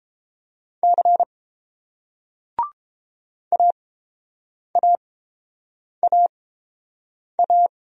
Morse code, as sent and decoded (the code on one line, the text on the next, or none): -.-. / .- .- .- .-
C AAAA